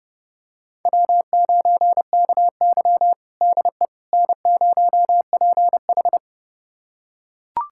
Morse code, heard as a keyed tone, about 30 wpm, 700 Hz. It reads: W9KY DE N0PH
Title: USA Callsign DE Callsign - 4 Characters - 30wpm